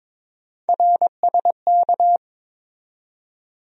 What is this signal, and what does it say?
Morse code, heard as a keyed tone, about 22 wpm, 700 Hz. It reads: RSK
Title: USA Call Sign Postfixes - 3 Letters 22wpm